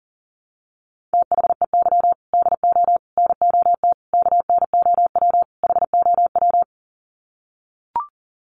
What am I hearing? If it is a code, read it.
Morse code, heard as a keyed tone, about 40 wpm, 700 Hz. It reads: THEY DO NOT KNOW HOW
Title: Sets of 5 Words Encore2 40wpm